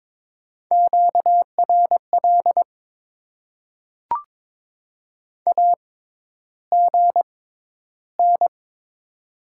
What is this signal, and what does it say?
Morse code, heard as a keyed tone, about 23 wpm, spaced as for 11 wpm, 700 Hz. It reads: QRL AGN
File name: Top 90 QSO Elements - 6x Character Spacing - 22wpm